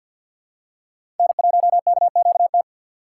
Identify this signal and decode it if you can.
Morse code, heard as a keyed tone, about 53 wpm, 700 Hz. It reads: N1KQT
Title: USA Call Signs - 5 Characters - 4x Character Spacing - 50wpm